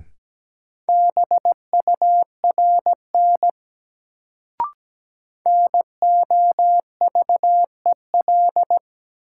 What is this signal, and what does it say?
Morse code, heard as a keyed tone, about 17 wpm, 700 Hz. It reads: BURN NOVEL
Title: Sentences from Top 1300 Words - New Word Drill - 17wpm